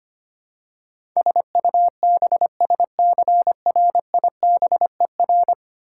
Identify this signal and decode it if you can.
Morse code, heard as a keyed tone, about 25 wpm, 700 Hz. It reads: SUBSCRIBER
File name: 10 Letter Words - First 3 Letters Given - 25wpm